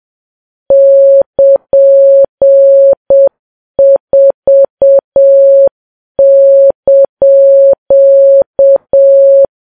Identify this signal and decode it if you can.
Morse code, heard as a keyed tone, about 7 wpm, 550 Hz. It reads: <KN>4)
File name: Files Web tls / Morse on9fv / morse7kps